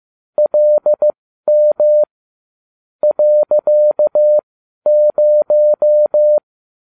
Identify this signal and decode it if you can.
Morse code, heard as a keyed tone, about 15 wpm, spaced as for 12 wpm, 600 Hz. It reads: LM .0